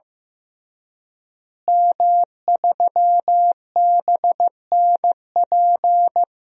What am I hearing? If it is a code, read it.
Morse code, heard as a keyed tone, about 15 wpm, 700 Hz. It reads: M3BNP
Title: DXCC-100 Call Signs - 5 Characters - Repeated 3x - 15wpm